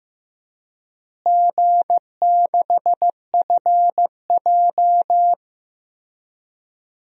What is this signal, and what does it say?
Morse code, heard as a keyed tone, about 15 wpm, 700 Hz. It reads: G6FJ
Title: DXCC-100 Call Signs - 4 Characters - Repeated 3x - 15wpm